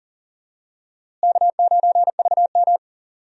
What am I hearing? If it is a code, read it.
Morse code, heard as a keyed tone, about 40 wpm, 700 Hz. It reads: K9VM
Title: USA Call Signs - 4 Characters - Repeated 3x - 40wpm